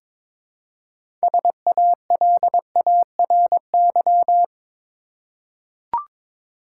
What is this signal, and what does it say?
Morse code, heard as a keyed tone, about 22 wpm, 700 Hz. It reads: SALARY
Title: Six Letter Words - 22wpm